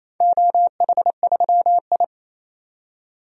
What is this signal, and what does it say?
Morse code, heard as a keyed tone, about 28 wpm, 700 Hz. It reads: OH3I
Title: DXCC-100 Call Signs - 4 Characters - 28wpm